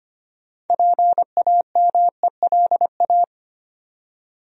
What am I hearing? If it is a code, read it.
Morse code, heard as a keyed tone, about 25 wpm, 700 Hz. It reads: PAMELA
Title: USA Common Names - 25wpm